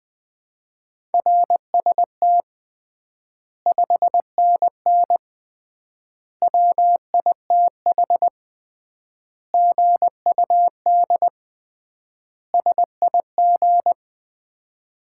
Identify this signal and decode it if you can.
Morse code, heard as a keyed tone, about 20 wpm, 700 Hz. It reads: RST 5NN WITH GUD SIG
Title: Common QSO Phrases - 3x Word-Spacing - 20wpm